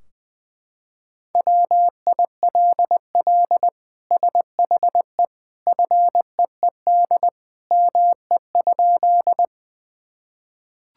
Morse code, heard as a keyed tone, about 20 wpm, 700 Hz. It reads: WILL SHE FEED ME?
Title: Sentences from Top 1600 Words 20wpm